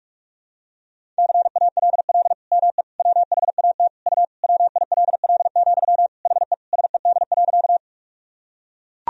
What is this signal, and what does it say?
Morse code, heard as a keyed tone, about 41 wpm, 700 Hz. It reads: CALL ME WHAT U WILL, HE SED.
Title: Sentences from Top 200 Words and Abbreviations 45wpm